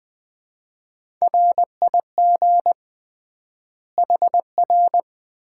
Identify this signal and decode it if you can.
Morse code, heard as a keyed tone, about 20 wpm, 700 Hz. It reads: RIG HR
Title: Common QSO Phrases - 3x Word-Spacing - 20wpm